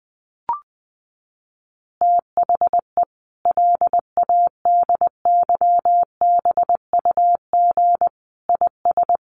Morse code, heard as a keyed tone, about 20 wpm, 700 Hz. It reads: THE LADYBUG IS
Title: Additional Easy Phrases 20wpm